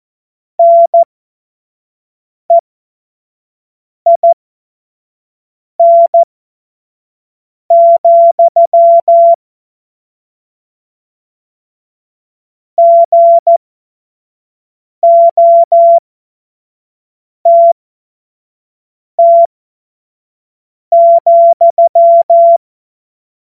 Morse code, heard as a keyed tone, about 14 wpm, spaced as for 5 wpm, 680 Hz. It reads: NEIN, GOTT,
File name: Morsekurs / Witze / Witz 01. August 2023 25 F70